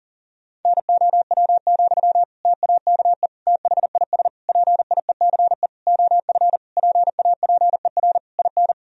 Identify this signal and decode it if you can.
Morse code, heard as a keyed tone, about 40 wpm, 700 Hz. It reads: NOW, TAKE THIS PIECE OF PAPER IN
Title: Sentences from Top 1000 Words - Review Part 2 - 40wpm